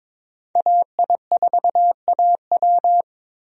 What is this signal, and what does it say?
Morse code, heard as a keyed tone, about 22 wpm, 700 Hz. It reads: AI4AW
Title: USA Call Signs - 5 Characters - Repeated 3x - 22wpm